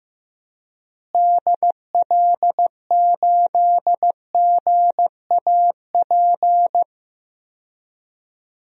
Morse code, heard as a keyed tone, about 15 wpm, 700 Hz. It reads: DL8GAP
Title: DXCC-100 Call Signs - 6 Characters - Repeated 2x - 15wpm